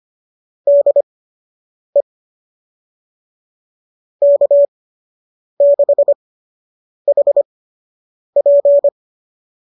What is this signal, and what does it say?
Morse code, heard as a keyed tone, about 25 wpm, spaced as for 8 wpm, 570 Hz. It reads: DE K6HP
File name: qso003 08